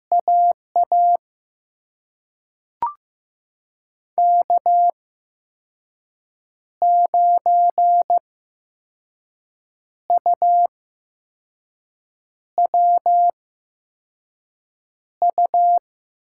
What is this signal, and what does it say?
Morse code, heard as a keyed tone, about 15 wpm, spaced as for 5 wpm, 700 Hz. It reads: AA K9UWU
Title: USA Call Signs - 5 Characters - 8x Character Spacing - 15wpm